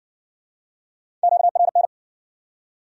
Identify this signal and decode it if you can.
Morse code, heard as a keyed tone, about 60 wpm, 700 Hz. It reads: QRN